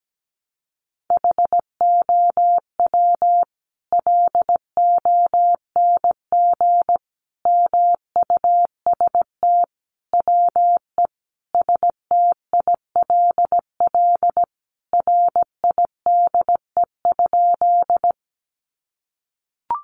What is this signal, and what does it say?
Morse code, heard as a keyed tone, about 17 wpm, 700 Hz. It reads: HOW LONG MUST WE STILL RIDE?
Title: Sentences from Top 1200 Words 17wpm